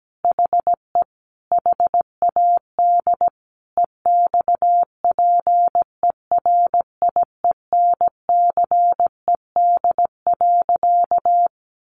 Morse code, heard as a keyed tone, about 17 wpm, 700 Hz. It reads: HE HAD EXPERIENCED.
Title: Sentences from Top 1100 Words 17wpm